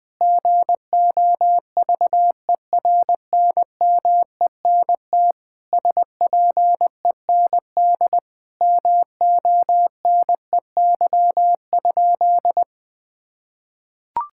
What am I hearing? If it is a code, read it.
Morse code, heard as a keyed tone, about 21 wpm, 700 Hz. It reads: GOVERNMENT SPEND MONEY?
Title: Sentences from Top 1000 Words - Review Part 5 - 20wpm